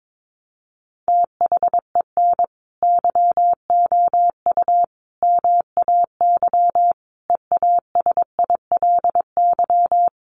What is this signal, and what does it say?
Morse code, heard as a keyed tone, about 22 wpm, 700 Hz. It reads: THEN YOU MAY EASILY
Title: Sentences from Top 1200 Words 22wpm